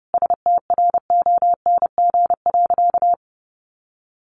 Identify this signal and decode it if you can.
Morse code, heard as a keyed tone, about 30 wpm, 700 Hz. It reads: STRONG.
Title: Sentences from Top 1400 Words 30wpm